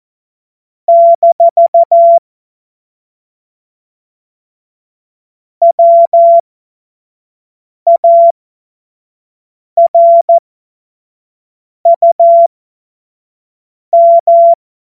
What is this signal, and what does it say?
Morse code, heard as a keyed tone, about 14 wpm, spaced as for 5 wpm, 680 Hz. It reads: - WARUM
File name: Morsekurs / Witze / Witz 23. Mai 2023 25 F70